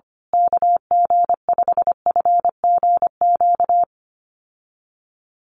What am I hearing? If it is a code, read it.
Morse code, heard as a keyed tone, about 25 wpm, 700 Hz. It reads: KG5FGQ
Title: USA Call Signs Repeated 3x 25wpm